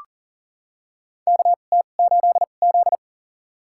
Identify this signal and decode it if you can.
Morse code, heard as a keyed tone, about 40 wpm, spaced as for 29 wpm, 700 Hz. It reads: KT8Z
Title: USA Call Signs - 4 Characters - 2x Character Spacing - 40wpm